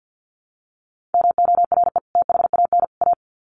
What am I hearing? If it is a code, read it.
Morse code, heard as a keyed tone, about 50 wpm, 700 Hz. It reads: MORE THAN A
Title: Sets of 3 Words 50wpm